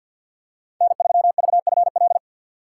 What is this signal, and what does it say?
Morse code, heard as a keyed tone, about 46 wpm, 700 Hz. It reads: N3VFL
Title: USA Call Signs - 5 Characters - 3x Character Spacing - 50wpm